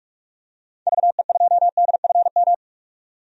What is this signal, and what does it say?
Morse code, heard as a keyed tone, about 46 wpm, 700 Hz. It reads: VE2DFM
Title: DXCC-100 Call Signs - 6 Characters - 45wpm